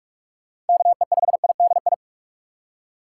Morse code, heard as a keyed tone, about 44 wpm, 700 Hz. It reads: KE5IDI
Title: USA Call Signs - 6 Characters - 2x Character Spacing - 45wpm